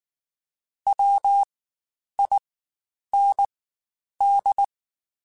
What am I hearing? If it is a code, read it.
Morse code, heard as a keyed tone, about 19 wpm, spaced as for 9 wpm, 800 Hz. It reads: WIND
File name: Spaced Characters / Spaced Characters-mp3s / Weather 4 21 sp @18